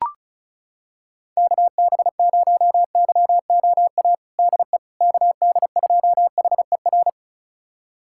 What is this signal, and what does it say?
Morse code, heard as a keyed tone, about 34 wpm, 700 Hz. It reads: KB0YOA DE KD2HER
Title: USA Callsign DE Callsign - 6 Characters - 35wpm